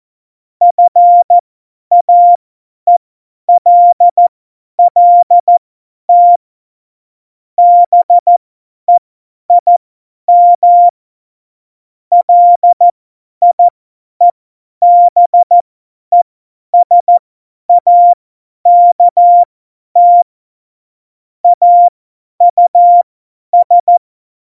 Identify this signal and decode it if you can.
Morse code, heard as a keyed tone, about 14 wpm, spaced as for 10 wpm, 700 Hz. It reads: FAELLT BEIM LIEBESAKT AUS
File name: Morsekurs / Witze / Witz 29. September 2020, 50 F70